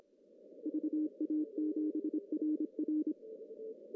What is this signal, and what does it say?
Morse code, heard as a keyed tone, about 26 wpm, 310 Hz. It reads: VA7RR
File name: CW / cqww-cw-k5tr